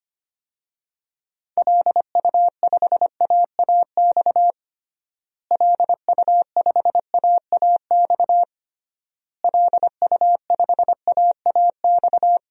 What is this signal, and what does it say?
Morse code, heard as a keyed tone, about 25 wpm, 700 Hz. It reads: LU5AAX LU5AAX LU5AAX
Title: DXCC-100 Call Signs - 6 Characters - Repeated 3x - 25wpm